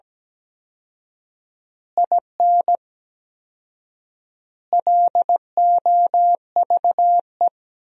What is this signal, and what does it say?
Morse code, heard as a keyed tone, about 17 wpm, 700 Hz. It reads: IN LOVE
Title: 4x Word-Spacing - Sentences from Top 400 Words - 17wpm